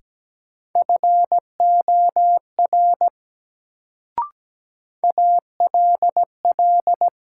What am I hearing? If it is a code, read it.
Morse code, heard as a keyed tone, about 17 wpm, 700 Hz. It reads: FOR ALL
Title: Sets of 2 Words - New Word Drill - 17wpm